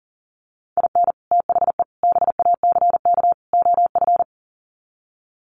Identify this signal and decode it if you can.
Morse code, heard as a keyed tone, about 40 wpm, 700 Hz. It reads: IN THE BACK OF
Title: Sets of 4 Words 40wpm